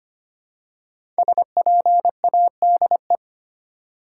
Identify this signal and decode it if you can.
Morse code, heard as a keyed tone, about 25 wpm, 690 Hz. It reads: SPADE
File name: practice / 25wpm / Easy Words with Hints - 25wpm